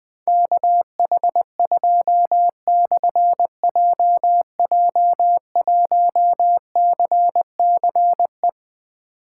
Morse code, heard as a keyed tone, about 20 wpm, 700 Hz. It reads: KH2/JJ1CCE
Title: Difficult Call Signs 20wpm